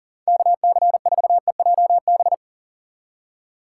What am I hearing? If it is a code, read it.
Morse code, heard as a keyed tone, about 38 wpm, 700 Hz. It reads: KC4EJB
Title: USA Call Signs - 6 Characters - 6x Character Spacing - 40wpm